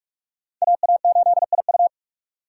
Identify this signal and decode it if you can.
Morse code, heard as a keyed tone, about 46 wpm, 690 Hz. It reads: AA8IU
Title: USA Call Signs - 5 Characters - 6x Character Spacing - 45wpm